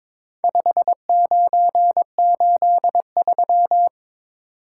Morse code, heard as a keyed tone, about 22 wpm, 700 Hz. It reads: 5983
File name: practice / 22wpm / Four Letter-Number Combinations - 22wpm